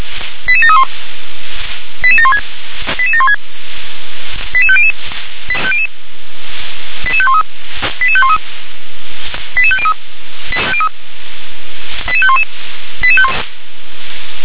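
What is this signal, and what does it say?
Five-tone selective call (ZVEI), heard as quick runs of five tones, tones of about 70 ms. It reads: 88731 88716 88716 88500 88513 88513 88503 88511
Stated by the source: ZVEI five-tone selective call received with an SDR.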